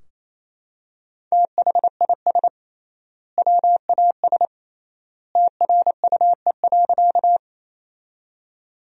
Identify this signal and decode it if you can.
Morse code, heard as a keyed tone, about 28 wpm, 700 Hz. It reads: THIS WAS TRUE.
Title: 3x Word-Spacing - Sentences from Top 400 Words - 28wpm